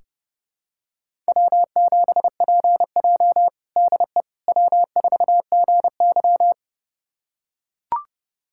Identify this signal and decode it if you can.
Morse code, heard as a keyed tone, about 30 wpm, 700 Hz. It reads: W7PJ DE W4GY
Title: USA Callsign DE Callsign - 4 Characters - 30wpm